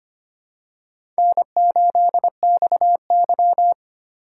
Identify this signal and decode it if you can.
Morse code, heard as a keyed tone, about 25 wpm, 700 Hz. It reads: N8XY
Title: USA Call Signs - 4 Characters - 8x Character Spacing - 25wpm